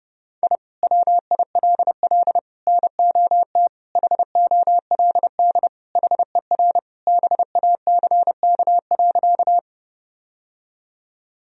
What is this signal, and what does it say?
Morse code, heard as a keyed tone, about 30 wpm, 700 Hz. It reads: I WILL NOT HOLD HER BACK.
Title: Sentences from Top 1000 Words - Review Part 2 - 30wpm